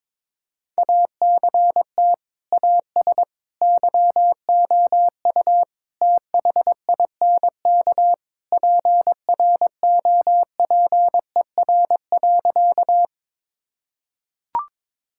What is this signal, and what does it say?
Morse code, heard as a keyed tone, about 21 wpm, 700 Hz. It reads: ACT AS YOU THINK PROPER.
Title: Sentences from Top 1000 Words - Review Part 5 - 22wpm